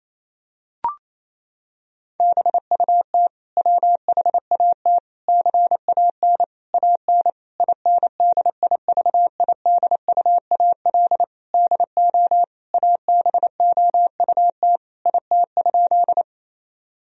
Morse code, heard as a keyed tone, about 28 wpm, 700 Hz. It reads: BUT WHAT CAN AN INDIVIDUAL DO ABOUT IT?
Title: Sentences from Top 1000 Words - Review Part 4 - 28wpm